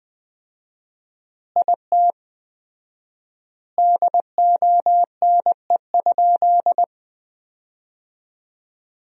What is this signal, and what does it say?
Morse code, heard as a keyed tone, about 20 wpm, 700 Hz. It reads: IT DONE?
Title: 4x Word-Spacing - Sentences from Top 500 Words - 20wpm